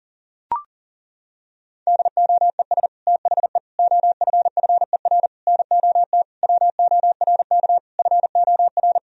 Morse code, heard as a keyed tone, about 37 wpm, 700 Hz. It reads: DOES THE OFFER NOT WORK FOR
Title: Sentences from Top 1000 Words - Review Part 4 - 40wpm